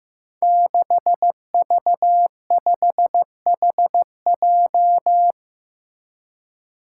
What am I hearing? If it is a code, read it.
Morse code, heard as a keyed tone, about 15 wpm, 700 Hz. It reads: 6V5HJ
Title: Five Challenging Letter-Number Combinations 15wpm